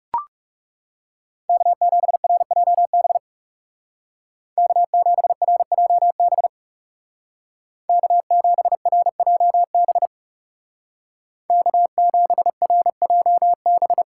Morse code, heard as a keyed tone, about 35 wpm, 700 Hz. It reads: K7RJB K7RJB K7RJB K7RJB
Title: Speed Racing - USA 1x3 Call Signs 30wpm